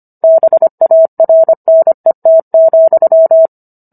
Morse code, heard as a keyed tone, about 25 wpm, 650 Hz. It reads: BARNET,